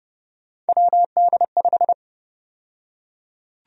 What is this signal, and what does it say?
Morse code, heard as a keyed tone, about 30 wpm, 700 Hz. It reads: WD5
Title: USA Call Sign Prefixes - 30wpm